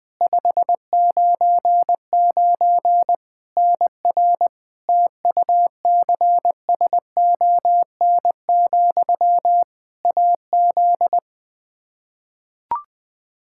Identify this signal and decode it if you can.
Morse code, heard as a keyed tone, about 20 wpm, 700 Hz. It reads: 599 NR TUCSON, AZ
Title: Practice QSO - Exchange 1 - Beginning - 20wpm